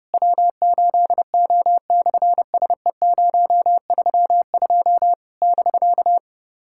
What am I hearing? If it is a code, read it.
Morse code, heard as a keyed tone, about 30 wpm, 700 Hz. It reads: W8O/SE032 <BK>
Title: SOTA QSO Practice 30wpm